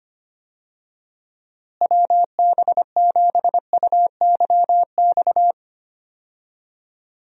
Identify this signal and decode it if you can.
Morse code, heard as a keyed tone, about 25 wpm, 700 Hz. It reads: WB7UYX
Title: USA Call Signs - 6 Characters - Repeated 3x - 25wpm